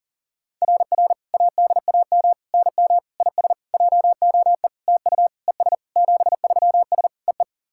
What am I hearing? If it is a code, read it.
Morse code, heard as a keyed tone, about 40 wpm, 690 Hz. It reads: RR ADAM NM IS JOE TU ES 73S EE
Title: SOTA QSO Practice 40wpm